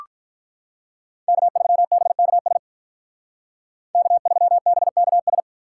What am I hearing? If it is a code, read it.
Morse code, heard as a keyed tone, about 44 wpm, 700 Hz. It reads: K3BKS K3BKS